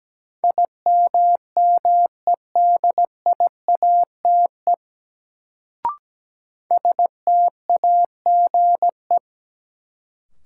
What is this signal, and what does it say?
Morse code, heard as a keyed tone, about 17 wpm, 700 Hz. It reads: IMMEDIATE STAGE
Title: Sentences from Top 1200 Words - New Word Drill Alt - 17wpm